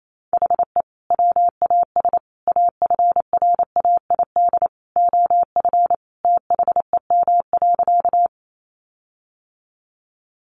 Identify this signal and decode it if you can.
Morse code, heard as a keyed tone, about 28 wpm, 700 Hz. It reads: HE WAS AFRAID OF THEM.
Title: Sentences from Top 1500 Words 28wpm